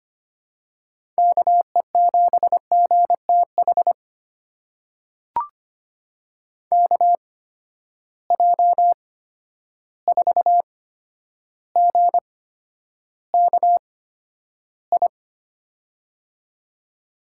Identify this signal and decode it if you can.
Morse code, heard as a keyed tone, about 25 wpm, spaced as for 11 wpm, 700 Hz. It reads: KE7GTH KJ4GKI
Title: USA Call Signs - 8x Character Spacing - 25wpm